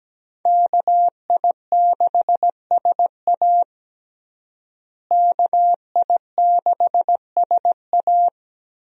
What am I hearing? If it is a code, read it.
Morse code, heard as a keyed tone, about 17 wpm, 700 Hz. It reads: KI6SA KI6SA
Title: USA Call Signs - 5 Characters - Repeated 2x - 17wpm